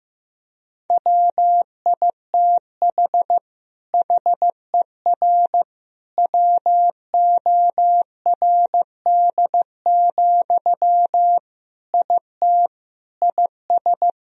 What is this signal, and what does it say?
Morse code, heard as a keyed tone, about 15 wpm, 700 Hz. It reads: WITH HER WORD, IT IS
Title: Sentences from Top 1000 Words - Review Part 2 - 15wpm